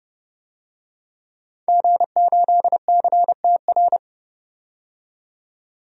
Morse code, heard as a keyed tone, about 32 wpm, 700 Hz. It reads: G8CTR
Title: DXCC-100 Call Signs - 5 Characters - 30wpm